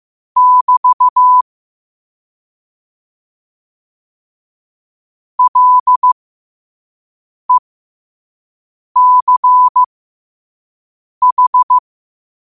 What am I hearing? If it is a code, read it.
Morse code, heard as a keyed tone, about 15 wpm, spaced as for 5 wpm, 1000 Hz. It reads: <BT> LECH
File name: qotdfiles / 5wpm